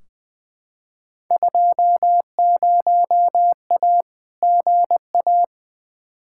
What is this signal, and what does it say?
Morse code, heard as a keyed tone, about 20 wpm, 700 Hz. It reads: 20A GA
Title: Field Day Exchanges Encore - 20wpm